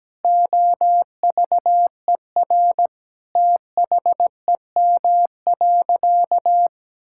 Morse code, heard as a keyed tone, about 17 wpm, 700 Hz. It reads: OVER THEM.
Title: Sentences from Top 1000 Words - Review Part 6 - 17wpm